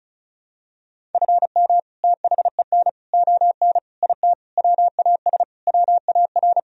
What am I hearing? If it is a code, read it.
Morse code, heard as a keyed tone, about 36 wpm, 700 Hz. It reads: FM THEN ON IT WAS WAR
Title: Sentences from Top 400 Words and Abbreviations 35wpm